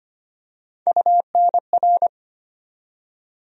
Morse code, heard as a keyed tone, about 25 wpm, 700 Hz. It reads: UNR
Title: USA Call Sign Postfixes - 3 Letters 25wpm